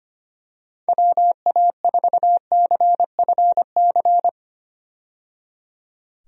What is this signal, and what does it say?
Morse code, heard as a keyed tone, about 25 wpm, 700 Hz. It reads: WA4CFC
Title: USA Call Signs - 6 Characters - 25wpm